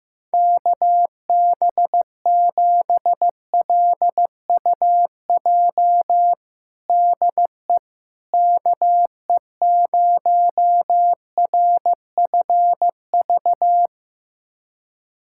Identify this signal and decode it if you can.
Morse code, heard as a keyed tone, about 15 wpm, 700 Hz. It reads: KB7LUJ DE KE0RFV
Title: USA Callsign DE Callsign - 6 Characters - 15wpm